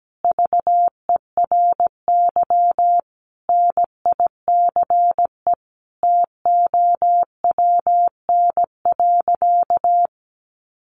Morse code, heard as a keyed tone, about 17 wpm, 700 Hz. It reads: VERY NICE TOWN.
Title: Sentences from Top 1100 Words 17wpm